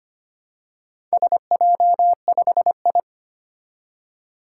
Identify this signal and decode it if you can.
Morse code, heard as a keyed tone, about 25 wpm, 700 Hz. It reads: SJ5I
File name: practice / 25wpm / Four Letter-Number Combinations - 25wpm